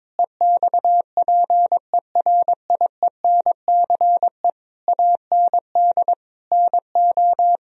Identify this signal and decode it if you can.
Morse code, heard as a keyed tone, about 21 wpm, 700 Hz. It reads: EXPERIENCE AND NO
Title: Sentences from Top 1000 Words - Review Part 4 - 22wpm